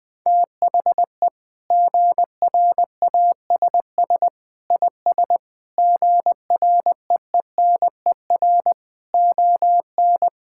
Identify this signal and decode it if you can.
Morse code, heard as a keyed tone, about 21 wpm, 700 Hz. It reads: THE GRASS IS GREENER ON
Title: Sentences with 4-Word Phrases - 20wpm